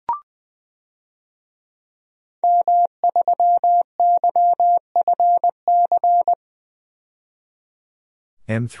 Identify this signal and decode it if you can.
Morse code, heard as a keyed tone, about 20 wpm, 700 Hz. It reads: M3YFC
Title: DXCC-100 Call Signs - 5 Characters - 20wpm